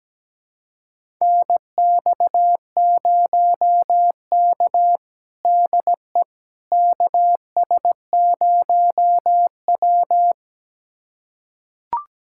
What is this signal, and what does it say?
Morse code, heard as a keyed tone, about 17 wpm, 700 Hz. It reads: NX0K DE KS0W
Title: USA Callsign DE Callsign - 4 Characters - 17wpm